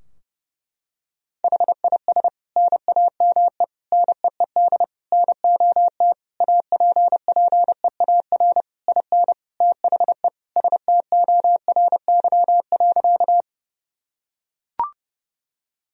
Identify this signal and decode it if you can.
Morse code, heard as a keyed tone, about 32 wpm, 700 Hz. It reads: HIS NAME NEED NOT APPEAR IN THE STORY.
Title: Sentences from Top 1000 Words - Review Part 2 - 30wpm